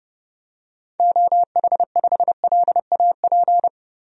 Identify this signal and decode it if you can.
Morse code, heard as a keyed tone, about 32 wpm, 700 Hz. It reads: OH5LAP
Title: DXCC-100 Call Signs - 6 Characters - Repeated 3x - 30wpm